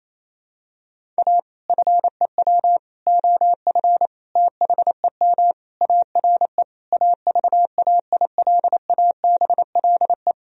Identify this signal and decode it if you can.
Morse code, heard as a keyed tone, about 28 wpm, 690 Hz. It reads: A FEW OF THEM ARE AVAILABLE
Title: Sentences from Top 1000 Words - Review Part 3 - 28wpm